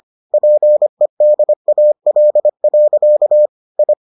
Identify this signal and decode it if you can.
Morse code, heard as a keyed tone, about 25 wpm, 600 Hz. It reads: PEDAL. I